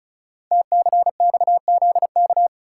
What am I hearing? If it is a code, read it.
Morse code, heard as a keyed tone, about 36 wpm, 700 Hz. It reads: TCXZK
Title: Code Groups - 5 Letters - 35wpm